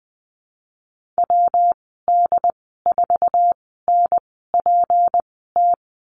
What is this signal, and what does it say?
Morse code, heard as a keyed tone, about 20 wpm, spaced as for 14 wpm, 700 Hz. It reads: WD4NPT
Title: 179 Lesson 29g - Slash - Call Signs 20wpm